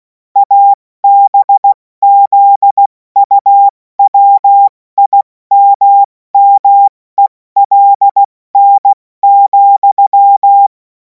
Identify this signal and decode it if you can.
Morse code, heard as a keyed tone, about 16 wpm, 800 Hz. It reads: ABZUWIMMELN,
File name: Morsekurs / Witze / Witz 24. März 2026 70 F80